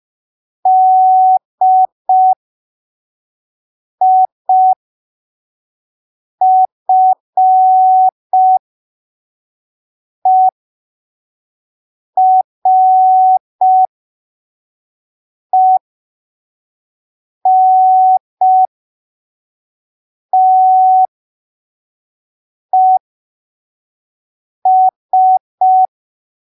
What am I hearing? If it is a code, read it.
Morse code, heard as a keyed tone, about 5 wpm, spaced as for 3 wpm, 750 Hz. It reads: DIFERENTES
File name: arquivos / texto / texto 12 05ppm